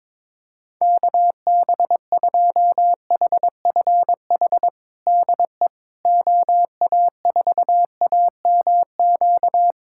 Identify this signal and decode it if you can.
Morse code, heard as a keyed tone, about 22 wpm, 700 Hz. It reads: KB2HFH DE OA4AMQ
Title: USA Callsign DE Callsign - 6 Characters - 22wpm